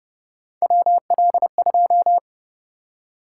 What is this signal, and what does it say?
Morse code, heard as a keyed tone, about 29 wpm, 700 Hz. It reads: WL2